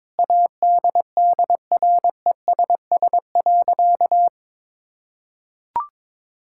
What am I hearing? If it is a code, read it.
Morse code, heard as a keyed tone, about 22 wpm, 700 Hz. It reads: ADDRESS.
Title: Sentences from Top 1000 Words - Review Part 6 - 22wpm